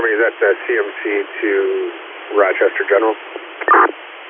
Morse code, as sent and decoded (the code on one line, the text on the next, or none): none